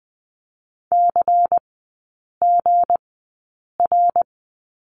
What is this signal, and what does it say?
Morse code, heard as a keyed tone, about 20 wpm, spaced as for 8 wpm, 700 Hz. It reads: CGR
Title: 184 Lesson 30e - Letter b - Three Characters 20wpm